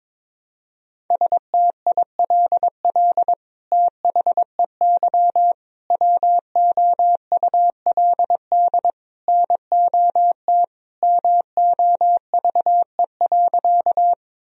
Morse code, heard as a keyed tone, about 22 wpm, 700 Hz. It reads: STILL THEY WOULD NOT MOVE.
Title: Sentences from Top 1000 Words - Review Part 1 - 22wpm